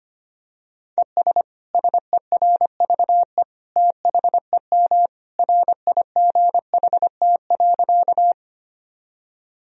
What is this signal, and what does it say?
Morse code, heard as a keyed tone, about 25 wpm, 700 Hz. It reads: ES SERVE THEM RIGHT.
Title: Sentences from Top 400 Words and Abbreviations 25wpm